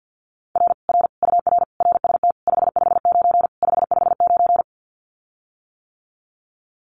Morse code, heard as a keyed tone, about 50 wpm, 700 Hz. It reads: R R UR RST 559 559
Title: Common QSO Phrases Encore 50wpm